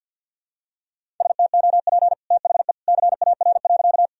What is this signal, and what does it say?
Morse code, heard as a keyed tone, about 51 wpm, 700 Hz. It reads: STOP THE CAR.
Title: Sentences from Top 1000 Words - Review Part 1 - 50wpm